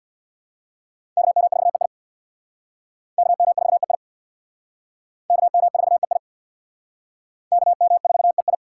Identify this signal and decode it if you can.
Morse code, heard as a keyed tone, about 60 wpm, 700 Hz. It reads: KM4EI KM4EI KM4EI KM4EI
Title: Speed Racing - USA 2x2 Call Signs 50wpm